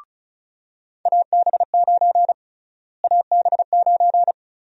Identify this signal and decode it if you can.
Morse code, heard as a keyed tone, about 36 wpm, 700 Hz. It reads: AB9 AB9
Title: USA Call Sign Prefixes - Repeated 2x - 35wpm